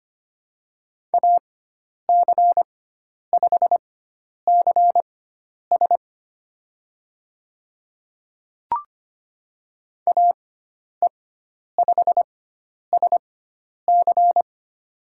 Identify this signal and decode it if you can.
Morse code, heard as a keyed tone, about 25 wpm, spaced as for 10 wpm, 700 Hz. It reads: AC5CS AE5SC
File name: Lesson 16g - Letter c - Call Signs 25wpm